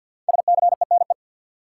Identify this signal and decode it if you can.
Morse code, heard as a keyed tone, about 52 wpm, 700 Hz. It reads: SCENE
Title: Sentences from Top 1100 Words - New Word Drill Alt - 50wpm